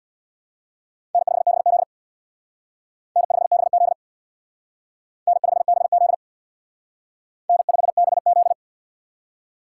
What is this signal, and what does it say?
Morse code, heard as a keyed tone, about 58 wpm, 700 Hz. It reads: N5BZ N5BZ N5BZ N5BZ
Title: Speed Racing - USA 1x2 Call Signs 50wpm